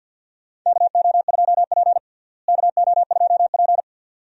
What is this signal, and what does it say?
Morse code, heard as a keyed tone, about 50 wpm, 700 Hz. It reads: KO2P KO2P
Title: USA Call Signs - 4 Characters - Repeated 2x - 50wpm